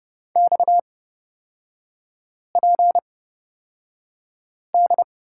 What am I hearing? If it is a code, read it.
Morse code, heard as a keyed tone, about 30 wpm, spaced as for 5 wpm, 700 Hz. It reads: XPD
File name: Lesson 37e - Letter x - Three Characters 30wpm